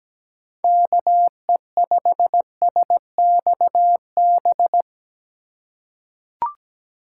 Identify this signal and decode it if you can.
Morse code, heard as a keyed tone, about 17 wpm, 700 Hz. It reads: KE5SXB
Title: USA Call Signs - 6 Characters - 3x Character Spacing - 17wpm